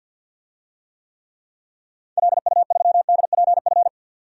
Morse code, heard as a keyed tone, about 49 wpm, 700 Hz. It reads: LU3DPF
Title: DXCC-100 Call Signs - 6 Characters - 50wpm